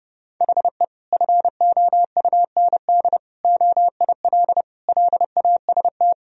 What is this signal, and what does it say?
Morse code, heard as a keyed tone, about 30 wpm, 700 Hz. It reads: HE FOUND OIL LAST
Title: Sentences from Top 1000 Words - Review Part 4 - 30wpm